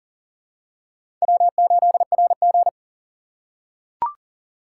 Morse code, heard as a keyed tone, about 38 wpm, 700 Hz. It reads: W8RG
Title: USA Call Signs - 4 Characters - 4x Character Spacing - 40wpm